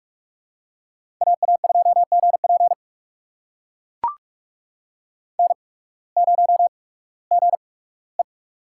Morse code, heard as a keyed tone, about 44 wpm, spaced as for 21 wpm, 690 Hz. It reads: AA2GP N0GE
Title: USA Call Signs - 8x Character Spacing - 45wpm